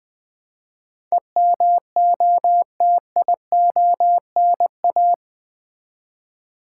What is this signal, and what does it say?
Morse code, heard as a keyed tone, about 20 wpm, 700 Hz. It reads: EMOTIONA
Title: Build A Word - 4 Letters At A Time - 11-Letter Words - 20wpm